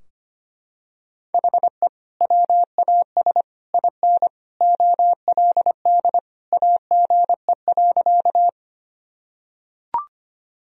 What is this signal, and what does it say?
Morse code, heard as a keyed tone, about 25 wpm, 700 Hz. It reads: HE WAS IN OLD AGE.
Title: Sentences from Top 1000 Words - Review Part 3 - 25wpm